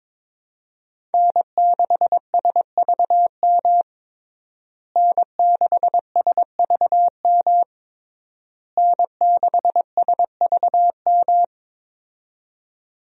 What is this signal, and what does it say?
Morse code, heard as a keyed tone, about 21 wpm, 700 Hz. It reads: N6SVM N6SVM N6SVM
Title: USA Call Signs - 5 Characters - Repeated 3x - 22wpm